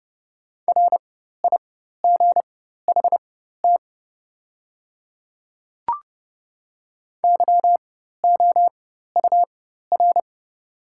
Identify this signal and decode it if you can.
Morse code, heard as a keyed tone, about 30 wpm, spaced as for 14 wpm, 700 Hz. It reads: RIGHT YOUR
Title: 4x Character Spacing - Top 200 Words - 30wpm